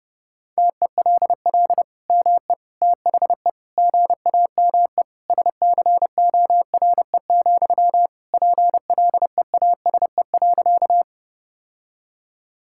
Morse code, heard as a keyed tone, about 30 wpm, 700 Hz. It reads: TELL ME THE GAME SCORE, PLEASE.
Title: Sentences from Top 1600 Words 30wpm